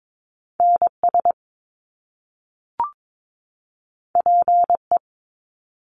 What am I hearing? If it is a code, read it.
Morse code, heard as a keyed tone, about 22 wpm, 700 Hz. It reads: NS PE
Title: Canadian Provinces and Territories 22wpm